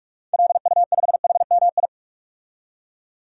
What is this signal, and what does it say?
Morse code, heard as a keyed tone, about 46 wpm, 700 Hz. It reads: LU5HMI
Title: DXCC-100 Call Signs - 6 Characters - 45wpm